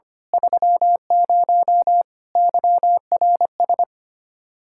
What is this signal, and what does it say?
Morse code, heard as a keyed tone, about 24 wpm, 700 Hz. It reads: 30 YRS
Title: Speed-Racing Common QSO Phrases Encore - 25wpm